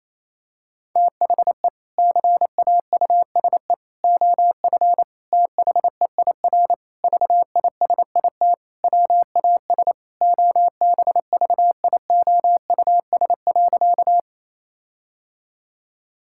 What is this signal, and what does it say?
Morse code, heard as a keyed tone, about 28 wpm, 700 Hz. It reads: THE CAUSE OF THEIR VISIT WAS OBVIOUS.
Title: Sentences from Top 1600 Words 28wpm